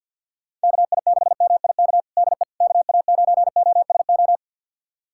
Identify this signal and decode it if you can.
Morse code, heard as a keyed tone, about 53 wpm, 700 Hz. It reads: KI6MIK DE KA9OSO
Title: USA Callsign DE Callsign - 6 Characters - 50wpm